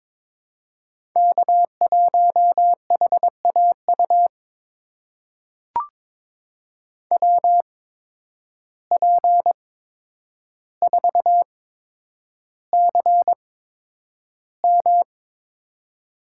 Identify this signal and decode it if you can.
Morse code, heard as a keyed tone, about 22 wpm, spaced as for 9 wpm, 700 Hz. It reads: K1HAU WP4CM
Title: USA Call Signs - 8x Character Spacing - 22wpm